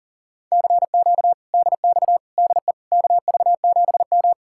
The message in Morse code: CQ DX DE KV7M

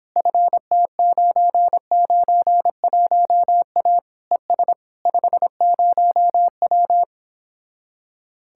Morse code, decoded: FT991A ES 50W

